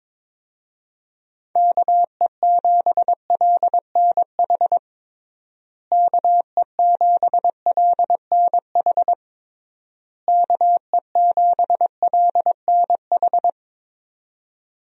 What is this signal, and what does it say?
Morse code, heard as a keyed tone, about 22 wpm, 700 Hz. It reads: KE7LNH KE7LNH KE7LNH
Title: USA Call Signs - 6 Characters - Repeated 3x - 22wpm